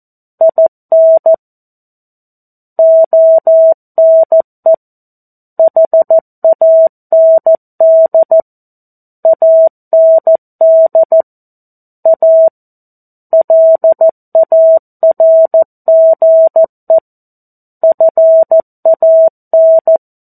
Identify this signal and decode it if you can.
Morse code, heard as a keyed tone, about 14 wpm, 650 Hz. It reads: IN ONE HAND AND A LARGE FAN
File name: 02-Chapter-02 14 WPM